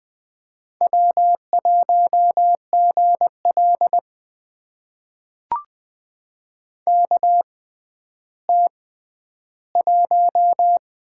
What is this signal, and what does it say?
Morse code, heard as a keyed tone, about 20 wpm, spaced as for 11 wpm, 700 Hz. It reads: W1GL KT1
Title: USA Call Signs - 4 Characters - 6x Character Spacing - 20wpm